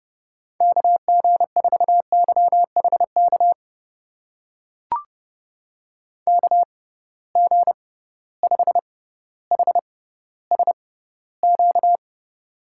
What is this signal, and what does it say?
Morse code, heard as a keyed tone, about 31 wpm, spaced as for 16 wpm, 700 Hz. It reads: KG4YHK KG5HSQ
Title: USA Call Signs - 6 Characters - 6x Character Spacing - 30wpm